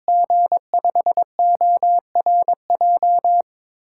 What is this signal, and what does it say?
Morse code, heard as a keyed tone, about 21 wpm, 700 Hz. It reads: G5ORJ
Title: USA Call Signs - 6 Characters - 8x Character Spacing - 22wpm